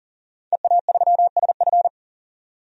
Morse code, heard as a keyed tone, about 39 wpm, 700 Hz. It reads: EA3SF
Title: DXCC-100 Call Signs - 5 Characters - 40wpm